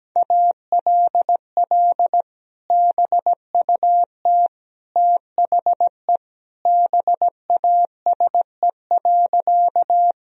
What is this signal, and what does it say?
Morse code, heard as a keyed tone, about 17 wpm, 700 Hz. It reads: ALL BUT THE BASE.
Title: Sentences from Top 1000 Words - Review Part 1 - 17wpm